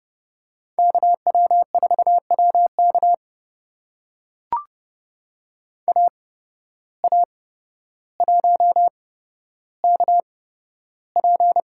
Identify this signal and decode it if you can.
Morse code, heard as a keyed tone, about 32 wpm, spaced as for 13 wpm, 700 Hz. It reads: KW4WK AA1KP
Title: USA Call Signs - 5 Characters - 8x Character Spacing - 30wpm